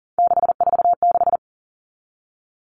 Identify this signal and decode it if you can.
Morse code, heard as a keyed tone, about 38 wpm, 700 Hz. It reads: 646
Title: Three Challenging Letter-Number Combinations 40wpm